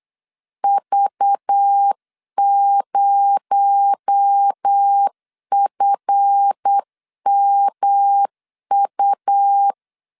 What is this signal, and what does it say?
Morse code, heard as a keyed tone, about 8 wpm, 800 Hz. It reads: V0FMU
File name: LEZIONI / 7 GF9Z / 7 GF 8wpm 3108